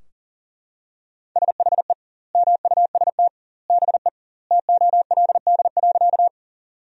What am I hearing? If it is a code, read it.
Morse code, heard as a keyed tone, about 40 wpm, 700 Hz. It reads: SHE MUST BE TOLD.
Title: 2x Word-Spacing - Sentences from Top 400 Words - 40wpm